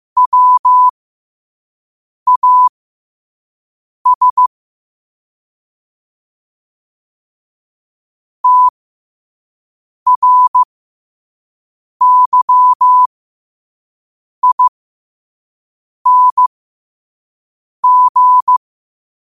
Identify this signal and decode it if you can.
Morse code, heard as a keyed tone, about 15 wpm, spaced as for 5 wpm, 1000 Hz. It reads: WAS TRYING
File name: qotdfiles / 5wpm